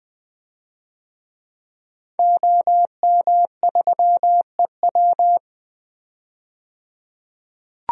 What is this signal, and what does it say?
Morse code, heard as a keyed tone, about 20 wpm, 700 Hz. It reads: OM3EW
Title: DXCC-100 Call Signs - 5 Characters - 20wpm